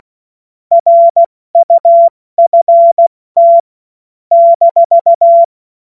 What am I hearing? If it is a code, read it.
Morse code, heard as a keyed tone, about 16 wpm, 680 Hz. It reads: RUFT -
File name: Morsekurs / Witze / Witz 28. September 2021, 70 F80